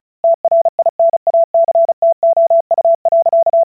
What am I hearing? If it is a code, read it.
Morse code, heard as a keyed tone, about 36 wpm, 670 Hz. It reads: TRINACTOU.
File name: vyuka cw / 35wpm / ukazka 35wpm